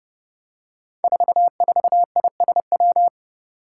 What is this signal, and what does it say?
Morse code, heard as a keyed tone, about 30 wpm, 700 Hz. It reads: 44ISW
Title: Five Challenging Letter-Number Combinations 30wpm